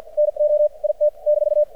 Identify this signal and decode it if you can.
Morse code, heard as a keyed tone, about 50 wpm, spaced as for 44 wpm, 600 Hz. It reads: NO IT -
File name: archived / CRSnet